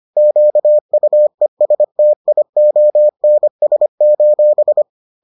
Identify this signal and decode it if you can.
Morse code, heard as a keyed tone, about 25 wpm, 600 Hz. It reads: QUESTIONS: